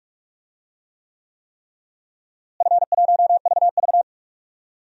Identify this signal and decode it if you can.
Morse code, heard as a keyed tone, about 41 wpm, 700 Hz. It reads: F1VV
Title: DXCC-100 Call Signs - 4 Characters - Repeated 3x - 45wpm